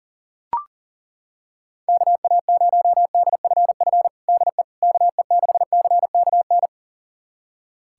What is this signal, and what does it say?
Morse code, heard as a keyed tone, about 40 wpm, 700 Hz. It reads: KA0DFF DE KE6CKN
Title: USA Callsign DE Callsign 40wpm